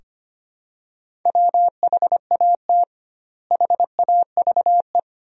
Morse code, heard as a keyed tone, about 25 wpm, 700 Hz. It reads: WHAT HAVE
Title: 2x Word-Spacing - Sentences from Top 200 Words - 25wpm